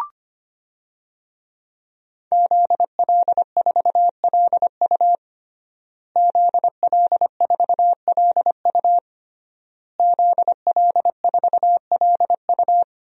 Morse code, heard as a keyed tone, about 25 wpm, 700 Hz. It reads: ZL4LU ZL4LU ZL4LU
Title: DXCC-100 Call Signs - 5 Characters - Repeated 3x - 25wpm